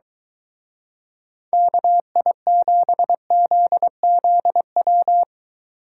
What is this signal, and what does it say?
Morse code, heard as a keyed tone, about 23 wpm, 700 Hz. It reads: KI7ZZW